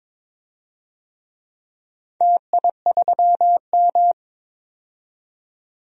Morse code, heard as a keyed tone, about 22 wpm, 700 Hz. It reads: TI3M
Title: DXCC-100 Call Signs - 4 Characters - 22wpm